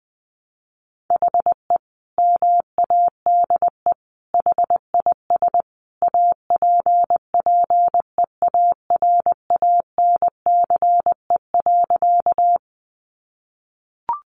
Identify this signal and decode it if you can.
Morse code, heard as a keyed tone, about 20 wpm, 700 Hz. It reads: HE MADE HIS APPEARANCE.
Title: Sentences from Top 1100 Words 20wpm